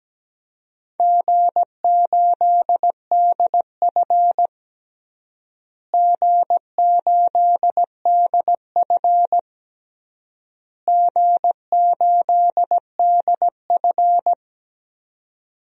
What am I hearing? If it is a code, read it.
Morse code, heard as a keyed tone, about 17 wpm, 700 Hz. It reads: G8DF G8DF G8DF
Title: DXCC-100 Call Signs - 4 Characters - Repeated 3x - 17wpm